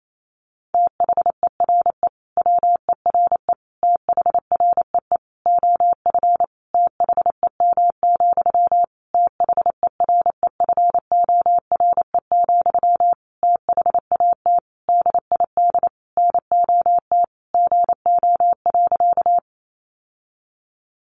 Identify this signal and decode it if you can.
Morse code, heard as a keyed tone, about 28 wpm, 700 Hz. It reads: THERE WERE THREE OF THEM, THEREFORE, THAT DID NOT GO.
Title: Sentences from Top 800 Words 28wpm